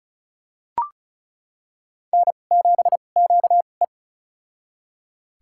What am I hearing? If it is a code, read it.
Morse code, heard as a keyed tone, about 36 wpm, spaced as for 25 wpm, 690 Hz. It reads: N7QE
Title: USA Call Signs - 4 Characters - 2x Character Spacing - 35wpm